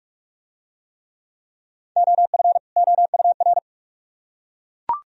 Morse code, heard as a keyed tone, about 41 wpm, 700 Hz. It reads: OF OUR